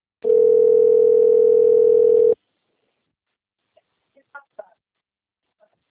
Ringback tone (North America), 440 + 480 Hz.